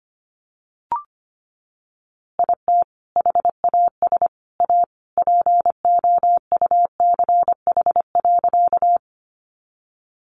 Morse code, heard as a keyed tone, about 25 wpm, 700 Hz. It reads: IT HAS A POUCH.
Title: Easy Phrases 25wpm